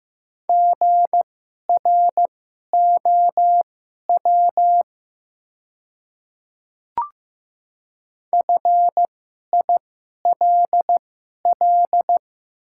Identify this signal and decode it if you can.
Morse code, heard as a keyed tone, about 15 wpm, spaced as for 11 wpm, 700 Hz. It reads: GROW FILL
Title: 2x Character Spacing - Top 500 Words - 15wpm